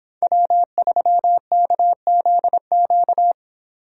Morse code, heard as a keyed tone, about 28 wpm, 700 Hz. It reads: W3KZQ